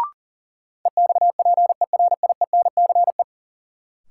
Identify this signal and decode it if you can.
Morse code, heard as a keyed tone, about 40 wpm, 700 Hz. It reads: EXPERIENCE
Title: Sentences from Top 600 Words - New Word Drill - 40wpm